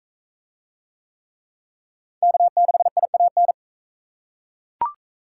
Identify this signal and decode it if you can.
Morse code, heard as a keyed tone, about 37 wpm, 700 Hz. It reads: K6IAN